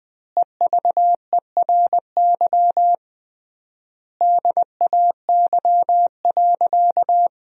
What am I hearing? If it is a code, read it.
Morse code, heard as a keyed tone, about 20 wpm, 700 Hz. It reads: EVERY DAY.
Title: 3x Word-Spacing - Sentences from Top 200 Words - 20wpm